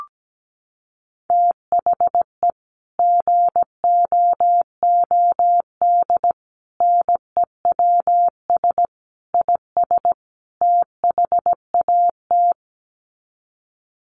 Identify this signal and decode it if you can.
Morse code, heard as a keyed tone, about 17 wpm, 700 Hz. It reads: THE GOOD NEWS IS THAT
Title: Sets of 5 Words 17wpm